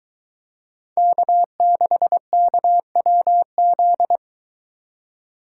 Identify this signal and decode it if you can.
Morse code, heard as a keyed tone, about 21 wpm, 700 Hz. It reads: K6KWZ